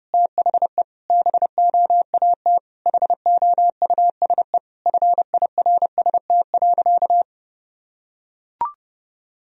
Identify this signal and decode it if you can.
Morse code, heard as a keyed tone, about 30 wpm, 700 Hz. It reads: THE BOAT HOUSE FIRST.
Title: Sentences from Top 1000 Words - Review Part 3 - 30wpm